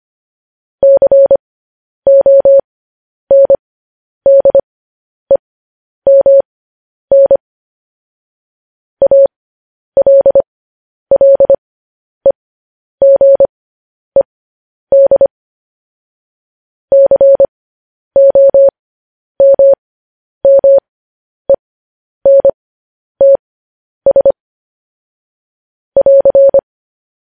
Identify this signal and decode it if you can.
Morse code, heard as a keyed tone, about 25 wpm, spaced as for 10 wpm, 550 Hz. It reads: CONDEMN ALLEGED COMMENTS <AR>
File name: podcast / News headlines-1x-25x10